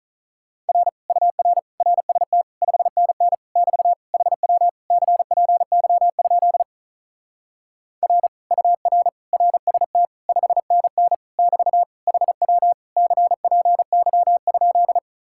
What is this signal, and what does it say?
Morse code, heard as a keyed tone, about 44 wpm, 700 Hz. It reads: R UR RST 5NN <BT> HW CPY? R UR RST 5NN <BT> HW CPY?